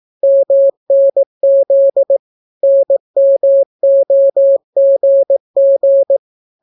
Morse code, heard as a keyed tone, about 18 wpm, 550 Hz. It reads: MNZ NMOGG